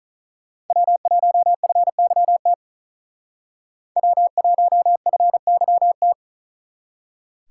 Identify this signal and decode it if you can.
Morse code, heard as a keyed tone, about 45 wpm, 700 Hz. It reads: W1FYT W1FYT